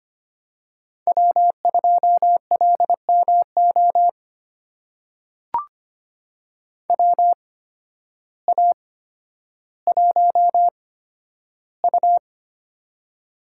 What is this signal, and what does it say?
Morse code, heard as a keyed tone, about 25 wpm, spaced as for 12 wpm, 700 Hz. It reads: W2LMO WA1U
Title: USA Call Signs - 8x Character Spacing - 25wpm